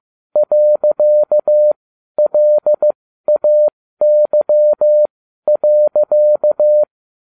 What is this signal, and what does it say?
Morse code, heard as a keyed tone, about 15 wpm, spaced as for 12 wpm, 600 Hz. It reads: .LAY.